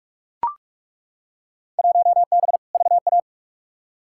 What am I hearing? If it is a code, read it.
Morse code, heard as a keyed tone, about 42 wpm, 700 Hz. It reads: 1B VA
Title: Field Day Exchanges Encore - 45wpm